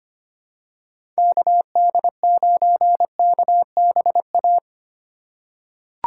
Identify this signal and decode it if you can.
Morse code, heard as a keyed tone, about 25 wpm, 700 Hz. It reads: KD9KBA
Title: USA Call Signs - 3x Character Spacing - 25wpm